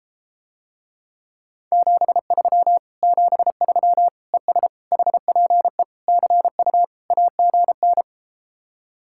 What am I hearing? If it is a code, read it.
Morse code, heard as a keyed tone, about 33 wpm, 700 Hz. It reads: 73 73 ES HPE CU AGN